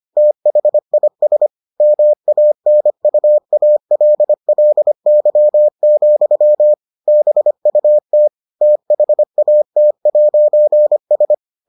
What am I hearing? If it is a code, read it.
Morse code, heard as a keyed tone, about 25 wpm, 600 Hz. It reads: THIS MANUALLY, BUT THAT'S